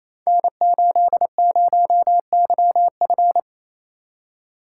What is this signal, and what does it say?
Morse code, heard as a keyed tone, about 27 wpm, 700 Hz. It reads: N80YF
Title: Code Groups - 5 Letter:Numbers - 28wpm